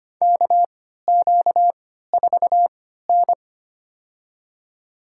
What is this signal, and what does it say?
Morse code, heard as a keyed tone, about 25 wpm, spaced as for 14 wpm, 700 Hz. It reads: KQ4N
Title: USA Call Signs - 4 Characters - 3x Character Spacing - 25wpm